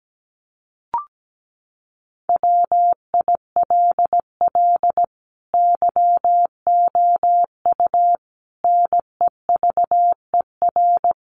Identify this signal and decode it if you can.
Morse code, heard as a keyed tone, about 17 wpm, 700 Hz. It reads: WILL YOU NEVER
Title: Sentences from Top 300 Words 17wpm